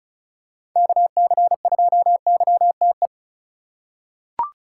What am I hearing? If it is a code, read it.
Morse code, heard as a keyed tone, about 36 wpm, 700 Hz. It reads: KC2YTE
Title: USA Call Signs - 8x Character Spacing - 35wpm